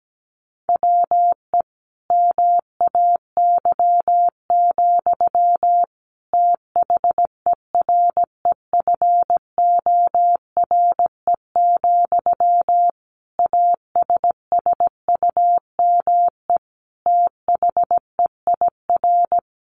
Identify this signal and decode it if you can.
Morse code, heard as a keyed tone, about 17 wpm, 700 Hz. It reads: WE MAY, THEREFORE, ASSUME THEIR
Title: Sentences from Top 1500 Words 17wpm